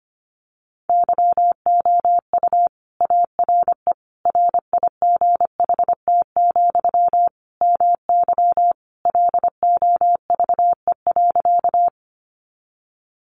Morse code, heard as a keyed tone, about 25 wpm, 700 Hz. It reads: YOU ARE RIGHT, MY LOVE.
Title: Sentences from Top 400 Words 25wpm